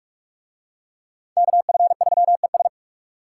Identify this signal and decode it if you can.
Morse code, heard as a keyed tone, about 41 wpm, 700 Hz. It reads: KF3ES